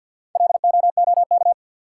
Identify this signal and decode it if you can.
Morse code, heard as a keyed tone, about 52 wpm, 700 Hz. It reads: LOOK
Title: Sets of 2 Words - New Word Drill - 50wpm